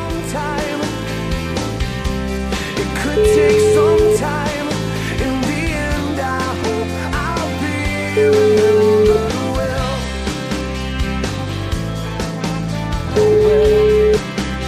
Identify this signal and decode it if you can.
Ringback tone (Europe, 425 Hz), with noise.